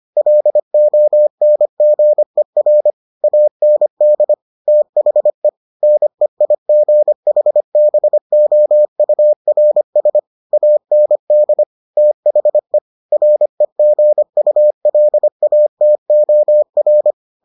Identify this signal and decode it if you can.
Morse code, heard as a keyed tone, about 25 wpm, 600 Hz. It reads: LONGER AND THE NEIGHBOURS AND THE REGULATOR